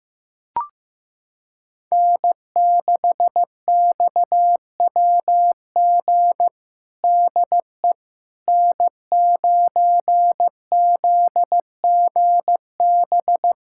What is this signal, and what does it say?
Morse code, heard as a keyed tone, about 15 wpm, 700 Hz. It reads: N6XWG DE N9ZGB
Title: USA Callsign DE Callsign - 5 Characters - 15wpm